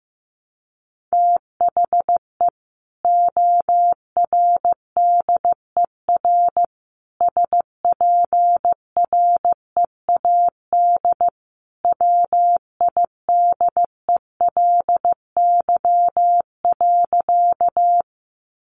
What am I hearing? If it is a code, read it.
Morse code, heard as a keyed tone, about 15 wpm, 700 Hz. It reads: THE ORDER SPREAD WIDELY.
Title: Sentences from Top 1200 Words 15wpm